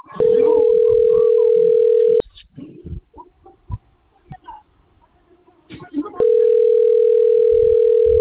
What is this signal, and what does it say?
Ringback tone (North America), 440 + 480 Hz, with speech.
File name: rg-1600-unknown- - - .54218